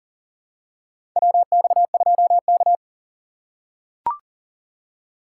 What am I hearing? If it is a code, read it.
Morse code, heard as a keyed tone, about 40 wpm, 700 Hz. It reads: WX2K
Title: USA Call Signs - 4 Characters - 3x Character Spacing - 40wpm